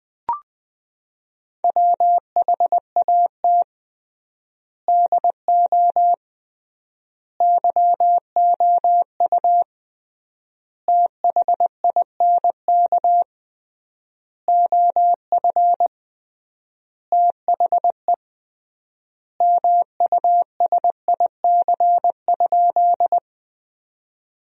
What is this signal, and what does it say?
Morse code, heard as a keyed tone, about 20 wpm, 700 Hz. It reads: WHAT DO YOU THINK OF THE MUSIC?
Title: 3x Word-Spacing - Sentences from Top 300 Words - 20wpm